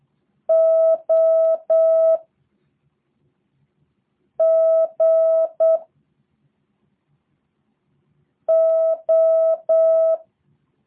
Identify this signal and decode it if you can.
Morse code, heard as a keyed tone, about 8 wpm, 650 Hz. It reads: O G O
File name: jh8cbh / 2023cw / review1-4